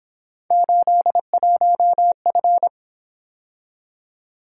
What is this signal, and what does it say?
Morse code, heard as a keyed tone, about 26 wpm, 700 Hz. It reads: 81F
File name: Speed-Racing - Three Letter-Number Combinations - 22wpm